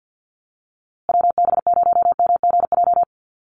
Morse code, heard as a keyed tone, about 50 wpm, 700 Hz. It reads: WB0MGJ
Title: USA Call Signs Repeated 2x 50wpm